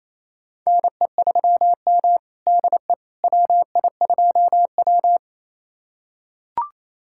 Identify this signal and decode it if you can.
Morse code, heard as a keyed tone, about 28 wpm, 700 Hz. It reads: NE3M DE WI2W
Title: USA Callsign DE Callsign - 4 Characters - 28wpm